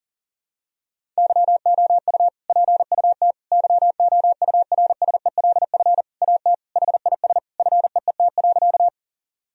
Morse code, heard as a keyed tone, about 40 wpm, 700 Hz. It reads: YOU PUT YOURSELF AT HIS FEET.
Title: Sentences from Top 1000 Words - Review Part 1 - 40wpm